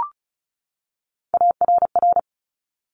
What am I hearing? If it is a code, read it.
Morse code, heard as a keyed tone, about 36 wpm, 690 Hz. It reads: ARR
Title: Top Three Letter Combinations in Words 35wpm